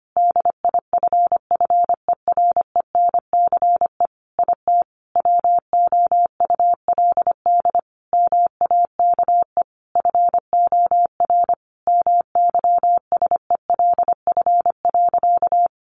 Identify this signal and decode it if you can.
Morse code, heard as a keyed tone, about 25 wpm, 700 Hz. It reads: DIFFERENCE IT WOULD MAKE FOR MYSELF.
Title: Sentences from Top 1300 Words 25wpm